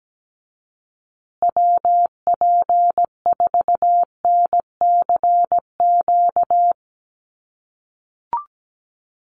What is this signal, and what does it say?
Morse code, heard as a keyed tone, about 17 wpm, 700 Hz. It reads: WP4NCQ
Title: USA Call Signs Repeated 2x 17wpm